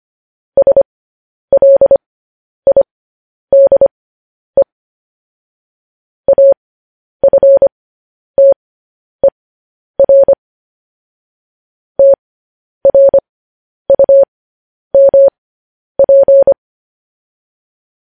Morse code, heard as a keyed tone, about 25 wpm, spaced as for 10 wpm, 550 Hz. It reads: SLIDE AFTER TRUMP